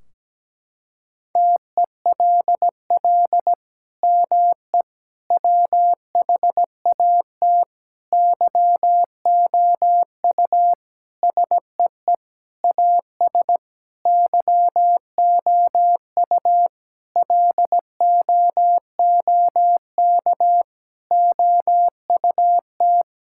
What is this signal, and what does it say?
Morse code, heard as a keyed tone, about 17 wpm, 700 Hz. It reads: TELL ME WHAT YOU SEE AS YOU LOOK OUT
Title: Sentences from Top 1600 Words 17wpm